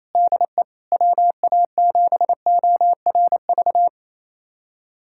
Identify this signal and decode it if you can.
Morse code, heard as a keyed tone, about 28 wpm, 700 Hz. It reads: DE WA7ORV
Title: USA Callsign DE Callsign - 6 Characters - 28wpm